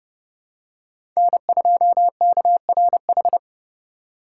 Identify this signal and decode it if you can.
Morse code, heard as a keyed tone, about 30 wpm, 690 Hz. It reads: N2KRH
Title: USA Call Signs - 5 Characters - 2x Character Spacing - 30wpm